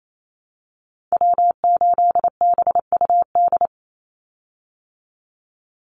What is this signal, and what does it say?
Morse code, heard as a keyed tone, about 28 wpm, 700 Hz. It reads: W8BUD
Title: Call Signs Repeated 2x - 28wpm